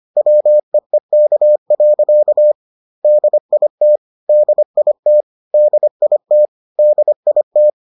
Morse code, heard as a keyed tone, about 25 wpm, 600 Hz. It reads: WEEK. DIT DIT DIT DIT